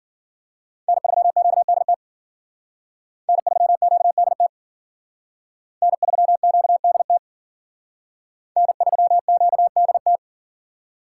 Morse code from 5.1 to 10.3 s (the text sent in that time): N3QDT N3QDT